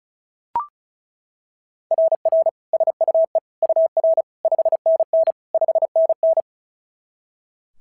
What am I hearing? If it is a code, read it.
Morse code, heard as a keyed tone, about 36 wpm, 690 Hz. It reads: RR SUE UR 5NN 5NN
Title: SOTA QSO Practice 35wpm